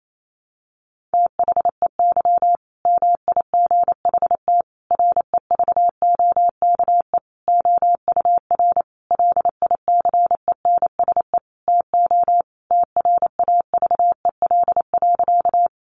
Morse code, heard as a keyed tone, about 28 wpm, 700 Hz. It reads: THEY MIGHT REVOKE OUR LICENSE TO TRAVEL.
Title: A Sound Of Thunder 28wpm